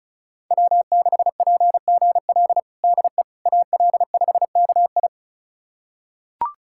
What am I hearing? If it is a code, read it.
Morse code, heard as a keyed tone, about 34 wpm, 700 Hz. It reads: W6PGL DE AL5KI
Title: USA Callsign DE Callsign - 5 Characters - 35wpm